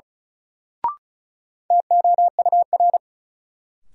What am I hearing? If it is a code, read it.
Morse code, heard as a keyed tone, about 34 wpm, 700 Hz. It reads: TOUR
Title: Sentences from Top 1400 Words - New Word Drill - 35wpm